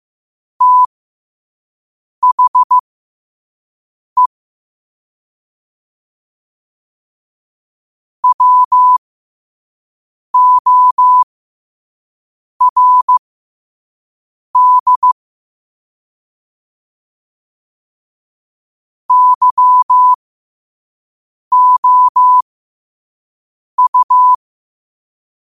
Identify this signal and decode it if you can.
Morse code, heard as a keyed tone, about 15 wpm, spaced as for 5 wpm, 1000 Hz. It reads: THE WORD YOU
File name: qotdfiles / 5wpm